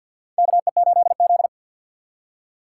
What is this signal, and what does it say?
Morse code, heard as a keyed tone, about 48 wpm, 700 Hz. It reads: KE8Z
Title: USA Call Signs - 4 Characters - 6x Character Spacing - 50wpm